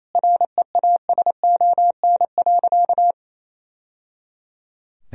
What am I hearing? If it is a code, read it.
Morse code, heard as a keyed tone, about 28 wpm, 700 Hz. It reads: REASON.
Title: Sentences from Top 1000 Words - Review Part 3 - 28wpm